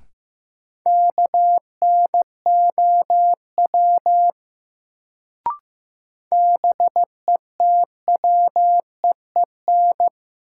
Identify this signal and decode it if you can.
Morse code, heard as a keyed tone, about 15 wpm, 700 Hz. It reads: KNOW BETWEEN
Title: Sets of 2 Words - New Word Drill - 15wpm